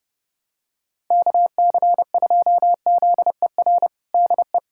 Morse code, heard as a keyed tone, about 30 wpm, 700 Hz. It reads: KC2ZER DE
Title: USA Callsign DE Callsign 30wpm